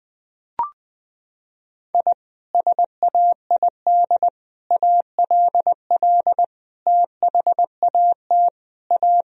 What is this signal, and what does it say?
Morse code, heard as a keyed tone, about 21 wpm, 700 Hz. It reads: I SAID ALL THAT A
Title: Sentences from Top 1000 Words - Review Part 2 - 20wpm